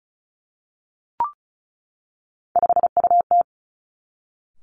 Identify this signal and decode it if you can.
Morse code, heard as a keyed tone, about 36 wpm, 700 Hz. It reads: 5UT
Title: Three Challenging Letter-Number Combinations 35wpm